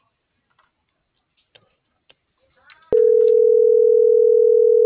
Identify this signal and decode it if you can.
Ringback tone (North America), 440 + 480 Hz.